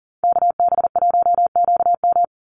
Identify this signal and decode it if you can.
Morse code, heard as a keyed tone, about 40 wpm, 700 Hz. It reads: KB1QM
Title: USA Call Signs 40wpm